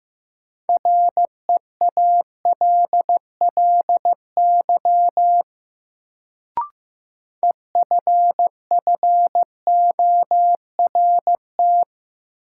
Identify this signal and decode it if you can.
Morse code, heard as a keyed tone, about 15 wpm, 700 Hz. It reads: REALLY EFFORT
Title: Sets of 4 Words - New Word Drill - 15wpm